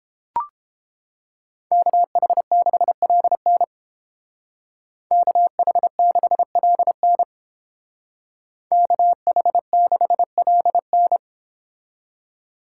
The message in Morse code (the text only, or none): KH6LN KH6LN KH6LN